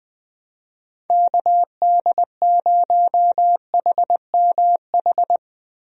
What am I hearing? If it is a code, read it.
Morse code, heard as a keyed tone, about 20 wpm, 700 Hz. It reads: KD0HMH
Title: USA Call Signs - 6 Characters - Repeated 2x - 20wpm